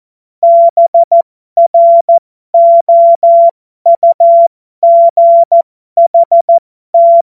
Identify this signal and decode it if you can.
Morse code, heard as a keyed tone, about 14 wpm, spaced as for 12 wpm, 680 Hz. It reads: BROUGHT
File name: Morsekurs / Witze